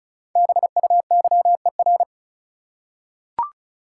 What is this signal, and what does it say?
Morse code, heard as a keyed tone, about 36 wpm, 700 Hz. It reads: BUYER
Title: Five Letter Words - 35wpm